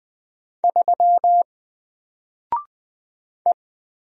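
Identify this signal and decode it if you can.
Morse code, heard as a keyed tone, about 20 wpm, 700 Hz. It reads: 3 E
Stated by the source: Single letter and number Morse drill at 20 wpm, from N4LCD.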